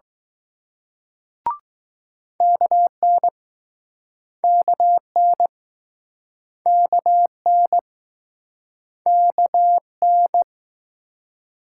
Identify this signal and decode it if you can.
Morse code, heard as a keyed tone, about 19 wpm, 700 Hz. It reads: KN KN KN KN